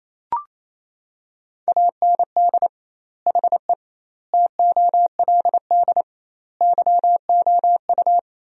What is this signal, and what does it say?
Morse code, heard as a keyed tone, about 28 wpm, 700 Hz. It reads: AND HE TOLD YOU
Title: 2x Word-Spacing - Sentences from Top 500 Words - 28wpm